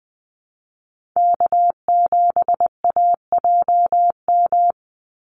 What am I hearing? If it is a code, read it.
Morse code, heard as a keyed tone, about 20 wpm, 700 Hz. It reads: K7AJM
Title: Call Signs - 20wpm